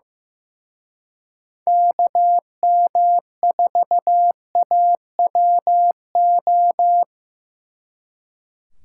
Morse code, heard as a keyed tone, about 18 wpm, 700 Hz. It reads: KM4AWO
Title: Speed Racing - USA 2x3 Call Signs 15wpm